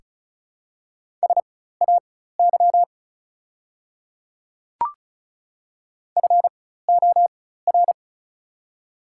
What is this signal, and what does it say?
Morse code, heard as a keyed tone, about 36 wpm, spaced as for 16 wpm, 700 Hz. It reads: SAY FOR
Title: 4x Character Spacing - Top 300 Words - 35wpm